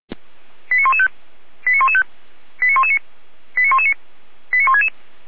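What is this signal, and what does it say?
Five-tone selective call (ZVEI), heard as quick runs of five tones, tones of about 80 ms. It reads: 89115 89115 89119 89119 89166
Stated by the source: ZVEI five-tone call received on SDR.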